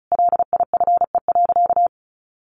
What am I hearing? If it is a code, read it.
Morse code, heard as a keyed tone, about 34 wpm, 700 Hz. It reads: LIFE.
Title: Sentences from Top 700 Words 35wpm